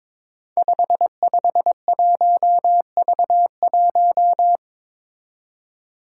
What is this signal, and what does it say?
Morse code, heard as a keyed tone, about 22 wpm, 700 Hz. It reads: 551V1
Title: Five Challenging Letter-Number Combinations 22wpm